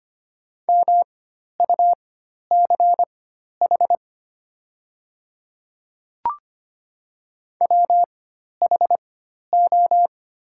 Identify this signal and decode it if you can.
Morse code, heard as a keyed tone, about 25 wpm, spaced as for 12 wpm, 700 Hz. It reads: MUCH WHO
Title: 4x Character Spacing - Top 300 Words - 25wpm